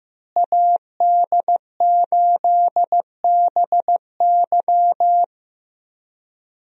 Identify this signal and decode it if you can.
Morse code, heard as a keyed tone, about 15 wpm, 700 Hz. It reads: AD8BY
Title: USA Call Signs - 5 Characters - Repeated 2x - 15wpm